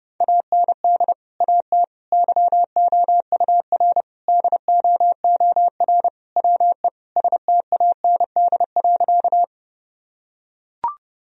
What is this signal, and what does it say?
Morse code, heard as a keyed tone, about 30 wpm, 700 Hz. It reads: AND AT YOUR DOOR WE STAND.
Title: Sentences from Top 1000 Words - Review Part 1 - 30wpm